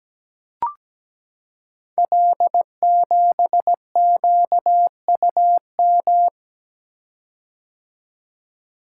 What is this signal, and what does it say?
Morse code, heard as a keyed tone, about 17 wpm, 700 Hz. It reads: L7QUM
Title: Code Groups - 5 Letter:Numbers - 17wpm